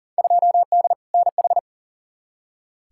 Morse code, heard as a keyed tone, about 40 wpm, 700 Hz. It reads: 2D NH
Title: Field Day Exchanges Encore - 40wpm